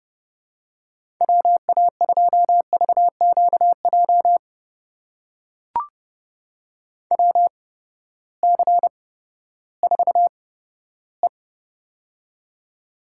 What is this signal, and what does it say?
Morse code, heard as a keyed tone, about 30 wpm, spaced as for 15 wpm, 700 Hz. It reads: WA2VQJ WC4E
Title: USA Call Signs - 8x Character Spacing - 30wpm